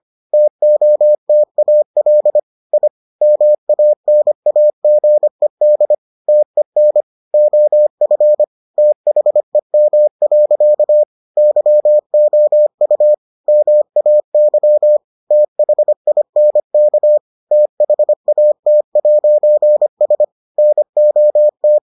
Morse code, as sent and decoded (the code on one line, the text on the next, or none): - --- - .- .-.. / .. / -- .- -. .- --. -.. / - . -. / --- ..-. / - .... -- .-.-.- / -.-- --- ..- / -- .- -.-- / - .... .. -. -.- / - .... .- - .----. ... / -. --- -
TOTAL I MANAGED TEN OF THEM. YOU MAY THINK THAT'S NOT